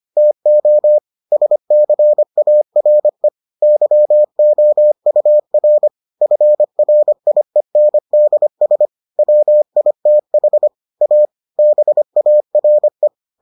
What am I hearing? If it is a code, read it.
Morse code, heard as a keyed tone, about 25 wpm, 600 Hz. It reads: TO SCARE YOUR FRIENDS WITH A BARE